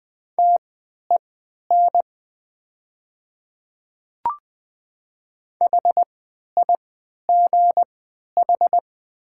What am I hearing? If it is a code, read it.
Morse code, heard as a keyed tone, about 20 wpm, spaced as for 11 wpm, 700 Hz. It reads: TEN HIGH
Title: 3x Character Spacing - Top 500 Words - 20wpm